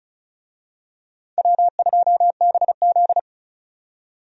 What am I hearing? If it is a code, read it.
Morse code, heard as a keyed tone, about 34 wpm, 700 Hz. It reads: W2BZ
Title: USA Call Signs - 4 Characters - 35wpm